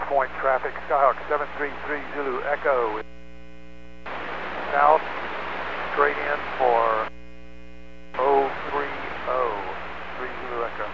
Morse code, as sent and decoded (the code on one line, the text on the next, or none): none